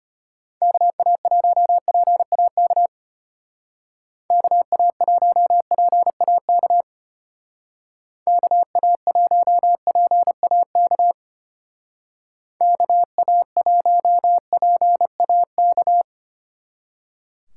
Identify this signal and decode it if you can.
Morse code, heard as a keyed tone, about 32 wpm, 700 Hz. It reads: KA1PAK KA1PAK KA1PAK KA1PAK